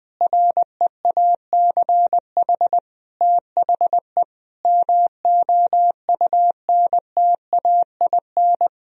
Morse code, heard as a keyed tone, about 20 wpm, 700 Hz. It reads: REACH THE MOUNTAIN
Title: Sentences from Top 1000 Words - Review Part 4 - 20wpm